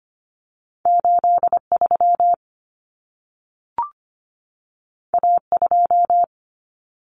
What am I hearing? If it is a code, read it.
Morse code, heard as a keyed tone, about 25 wpm, 700 Hz. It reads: 83 A2
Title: Common Cut Numbers - 2-Digit 25wpm